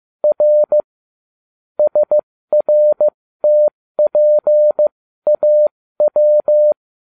Morse code, heard as a keyed tone, about 15 wpm, spaced as for 13 wpm, 600 Hz. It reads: R SRTPAW